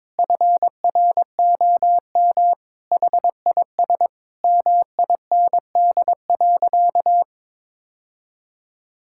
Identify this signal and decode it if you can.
Morse code, heard as a keyed tone, about 22 wpm, 700 Hz. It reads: FROM HIS MIND.
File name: practice / 22wpm / Sentences with 2-Word Phrases - 22wpm